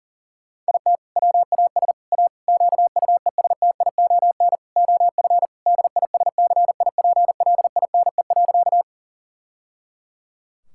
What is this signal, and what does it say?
Morse code, heard as a keyed tone, about 40 wpm, 700 Hz. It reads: IT WAS A QUESTION OF DISCIPLINE.
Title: Sentences from Top 1000 Words - Review Part 6 - 40wpm